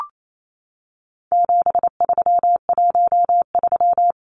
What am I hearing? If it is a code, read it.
Morse code, heard as a keyed tone, about 28 wpm, 700 Hz. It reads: 7313
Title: Numbers - 4-Digit 28wpm